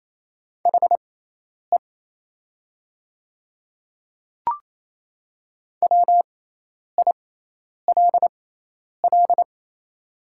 Morse code, heard as a keyed tone, about 28 wpm, spaced as for 10 wpm, 700 Hz. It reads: HE WILL
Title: 6x Character Spacing - Top 200 Words - 28wpm